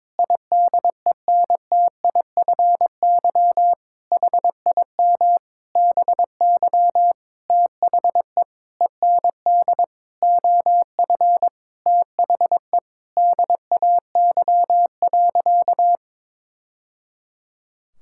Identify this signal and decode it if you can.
Morse code, heard as a keyed tone, about 22 wpm, 700 Hz. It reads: IDENTIFY HIM BY THE END OF THE DAY.
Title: Sentences from Top 1000 Words - Review Part 6 - 22wpm